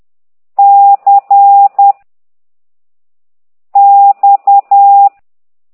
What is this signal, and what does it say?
Morse code, heard as a keyed tone, about 10 wpm, 800 Hz. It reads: C X
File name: exerc7